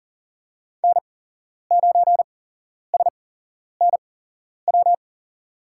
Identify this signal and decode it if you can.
Morse code, heard as a keyed tone, about 40 wpm, spaced as for 11 wpm, 700 Hz. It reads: N9SNW
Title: USA Call Signs - 5 Characters - 8x Character Spacing - 40wpm